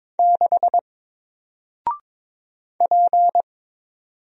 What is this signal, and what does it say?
Morse code, heard as a keyed tone, about 22 wpm, 700 Hz. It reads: 6 P
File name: Single-Letter-Number-22wpm